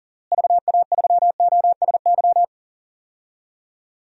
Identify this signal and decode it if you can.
Morse code, heard as a keyed tone, about 40 wpm, 700 Hz. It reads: VA3OSY
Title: DXCC-100 Call Signs - 6 Characters - Repeated 2x - 40wpm